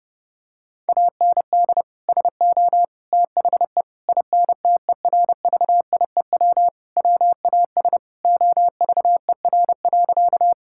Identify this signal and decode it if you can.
Morse code, heard as a keyed tone, about 30 wpm, 700 Hz. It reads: AND SO THE INTERVIEW WAS OVER.
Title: Sentences from Top 1600 Words 30wpm